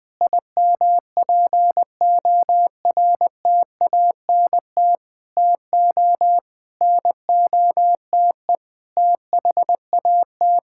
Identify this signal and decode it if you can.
Morse code, heard as a keyed tone, about 20 wpm, 700 Hz. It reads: IMPORTANT TO NOTE THAT
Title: Sentences with 6-Word Phrases - 20wpm